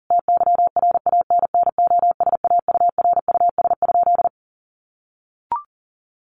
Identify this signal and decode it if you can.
Morse code, heard as a keyed tone, about 40 wpm, 700 Hz. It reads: TYRANNOSAURUS?
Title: A Sound Of Thunder 40wpm